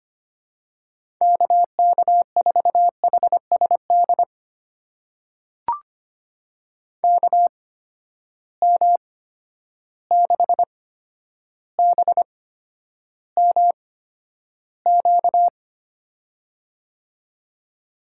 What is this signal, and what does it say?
Morse code, heard as a keyed tone, about 25 wpm, spaced as for 11 wpm, 700 Hz. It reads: KK4HSD KM6BMQ
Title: USA Call Signs - 8x Character Spacing - 25wpm